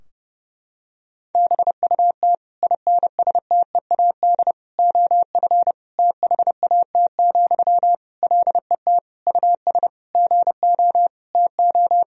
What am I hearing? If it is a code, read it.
Morse code, heard as a keyed tone, about 30 wpm, 700 Hz. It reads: BUT INSTEAD OF THAT, LET US GO TO
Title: Sentences from Top 1000 Words - Review Part 4 - 30wpm